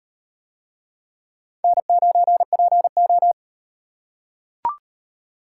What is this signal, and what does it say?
Morse code, heard as a keyed tone, about 38 wpm, 700 Hz. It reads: N9PO